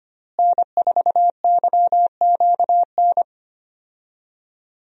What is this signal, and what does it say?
Morse code, heard as a keyed tone, about 25 wpm, 700 Hz. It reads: N4YQN
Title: USA Call Signs - 5 Characters - Repeated 3x - 25wpm